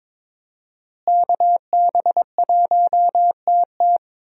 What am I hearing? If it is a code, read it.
Morse code, heard as a keyed tone, about 22 wpm, 700 Hz. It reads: KB1TT
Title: USA Call Signs - 5 Characters - Repeated 2x - 22wpm